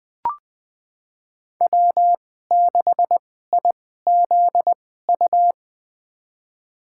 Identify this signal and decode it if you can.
Morse code, heard as a keyed tone, about 20 wpm, spaced as for 15 wpm, 700 Hz. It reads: W6IZU
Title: USA Call Signs - 5 Characters - 2x Character Spacing - 20wpm